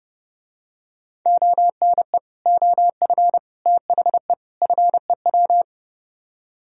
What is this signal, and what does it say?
Morse code, heard as a keyed tone, about 30 wpm, 700 Hz. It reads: ONE OF THE FEW